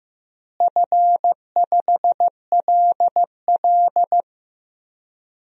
Morse code, heard as a keyed tone, about 15 wpm, 700 Hz. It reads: F5LL
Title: DXCC-100 Call Signs - 4 Characters - Repeated 2x - 15wpm